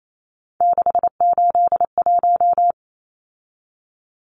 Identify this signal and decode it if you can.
Morse code, heard as a keyed tone, about 28 wpm, 700 Hz. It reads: 681
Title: Numbers - 3-Digit 28wpm